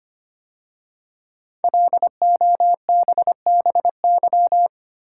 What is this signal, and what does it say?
Morse code, heard as a keyed tone, about 25 wpm, 700 Hz. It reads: LOBBY
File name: 5 Letter Words - First 3 Letters Given - 25wpm